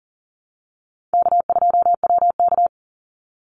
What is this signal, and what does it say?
Morse code, heard as a keyed tone, about 40 wpm, 700 Hz. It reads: K2WK
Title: Easy Call Signs Repeated 2x - 40wpm